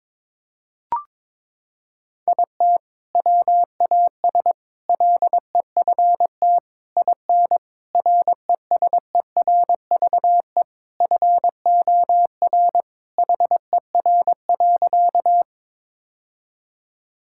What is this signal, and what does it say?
Morse code, heard as a keyed tone, about 21 wpm, 700 Hz. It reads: IT WAS LEFT IN RESERVE FOR HER.
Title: Sentences from Top 1000 Words - Review Part 6 - 22wpm